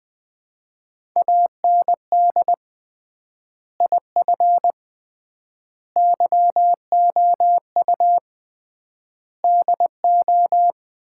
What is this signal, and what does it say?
Morse code, heard as a keyed tone, about 19 wpm, 700 Hz. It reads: AND IF YOU DO
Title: 3x Word-Spacing - Sets of 5 Words - 20wpm